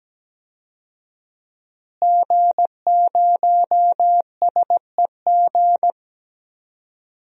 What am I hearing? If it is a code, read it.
Morse code, heard as a keyed tone, about 17 wpm, 700 Hz. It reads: G0SEG
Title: DXCC-100 Call Signs - 5 Characters - Repeated 3x - 17wpm